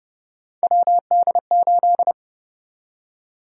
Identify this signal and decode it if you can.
Morse code, heard as a keyed tone, about 30 wpm, 700 Hz. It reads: WD8
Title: USA Call Sign Prefixes - Repeated 2x - 30wpm